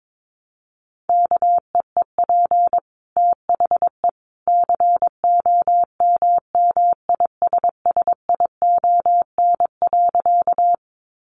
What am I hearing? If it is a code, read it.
Morse code, heard as a keyed tone, about 22 wpm, 700 Hz. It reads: KEEP THE COMMISSION.
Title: Sentences from Top 1400 Words 22wpm